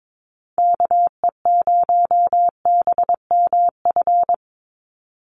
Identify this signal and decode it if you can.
Morse code, heard as a keyed tone, about 22 wpm, 700 Hz. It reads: KE0BMF
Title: USA Call Signs 22wpm